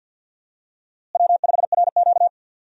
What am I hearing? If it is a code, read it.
Morse code, heard as a keyed tone, about 49 wpm, 700 Hz. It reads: W5RQ
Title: USA Call Signs - 4 Characters - Repeated 2x - 50wpm